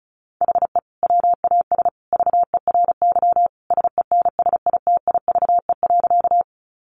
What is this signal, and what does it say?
Morse code, heard as a keyed tone, about 34 wpm, 700 Hz. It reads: HE WAS VERY SENSITIVE.
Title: Sentences from Top 1400 Words 35wpm